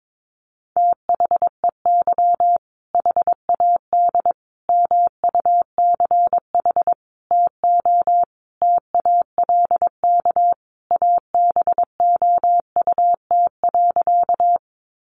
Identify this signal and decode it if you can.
Morse code, heard as a keyed tone, about 22 wpm, 700 Hz. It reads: THEY HAD MUCH TO TALK ABOUT.
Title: Sentences from Top 400 Words 22wpm